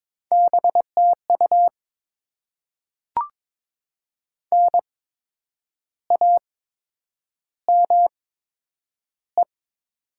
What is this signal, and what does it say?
Morse code, heard as a keyed tone, about 22 wpm, spaced as for 9 wpm, 700 Hz. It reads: BTU NAME
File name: Top 20 QSO Elements - 8x Character Spacing - 22wpm